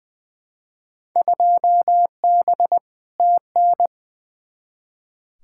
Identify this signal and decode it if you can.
Morse code, heard as a keyed tone, about 20 wpm, 700 Hz. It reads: 2B TN
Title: Field Day Exchanges Encore - 20wpm